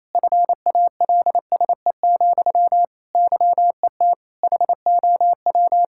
Morse code, heard as a keyed tone, about 28 wpm, 700 Hz. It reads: FALSE, YET HOW
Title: Sentences from Top 1600 Words 28wpm